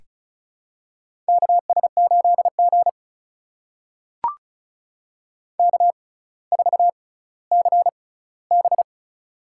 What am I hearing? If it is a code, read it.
Morse code, heard as a keyed tone, about 36 wpm, spaced as for 18 wpm, 700 Hz. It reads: KS8G K4CB
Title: USA Call Signs - 4 Characters - 6x Character Spacing - 35wpm